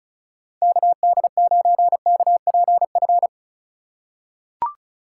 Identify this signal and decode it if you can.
Morse code, heard as a keyed tone, about 34 wpm, 700 Hz. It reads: KD9KPF
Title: USA Call Signs - 6 Characters - 3x Character Spacing - 35wpm